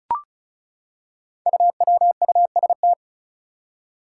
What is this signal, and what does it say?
Morse code, heard as a keyed tone, about 36 wpm, 700 Hz. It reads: UWUST
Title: Five Challenging Letter-Number Combinations 35wpm